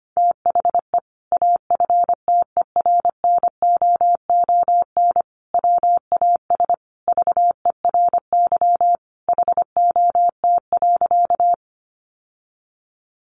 Morse code, heard as a keyed tone, about 25 wpm, 700 Hz. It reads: THE AFTERNOON WAS VERY HOT.
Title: Sentences from Top 1500 Words 25wpm